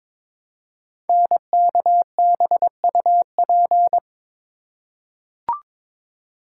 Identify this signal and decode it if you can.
Morse code, heard as a keyed tone, about 22 wpm, 700 Hz. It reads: NKBUP
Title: Code Groups - 5 Letters - 22wpm